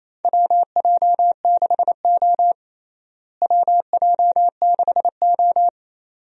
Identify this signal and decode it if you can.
Morse code, heard as a keyed tone, about 28 wpm, 700 Hz. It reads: WJ6O WJ6O
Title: USA Call Signs - 4 Characters - Repeated 2x - 28wpm